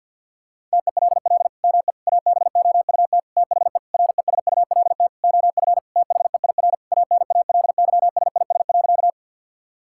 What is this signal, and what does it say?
Morse code, heard as a keyed tone, about 48 wpm, 700 Hz. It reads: TELL ME ABOUT THE RESULT OF THEIR ANALYSIS.
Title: Sentences from Top 1000 Words - Review Part 5 - 50wpm